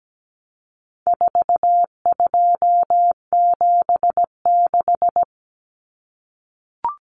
Morse code, heard as a keyed tone, about 17 wpm, 700 Hz. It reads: 4276
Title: Common Cut Numbers - 4-Digit 17wpm